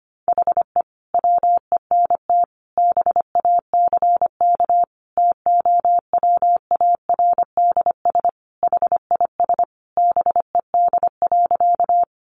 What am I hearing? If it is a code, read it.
Morse code, heard as a keyed tone, about 25 wpm, 700 Hz. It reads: HE WENT BACK TOWARDS HIS BED.
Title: Sentences from Top 1100 Words 25wpm